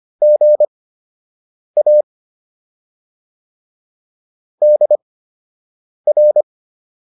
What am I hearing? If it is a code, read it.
Morse code, heard as a keyed tone, about 25 wpm, spaced as for 7 wpm, 600 Hz. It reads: GA DR